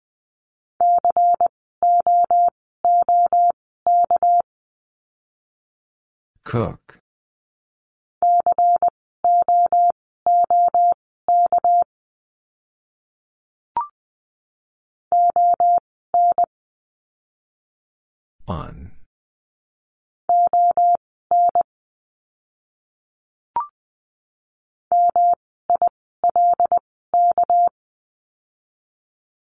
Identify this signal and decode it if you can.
Morse code, heard as a keyed tone, about 20 wpm, spaced as for 14 wpm, 700 Hz. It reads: COOK COOK ON ON MILK
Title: 199 Lesson 32f - Letter k - Words 20wpm